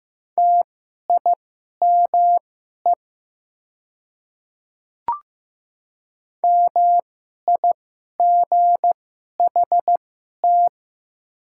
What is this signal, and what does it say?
Morse code, heard as a keyed tone, about 15 wpm, spaced as for 11 wpm, 700 Hz. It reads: TIME MIGHT
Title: 2x Character Spacing - Top 300 Words - 15wpm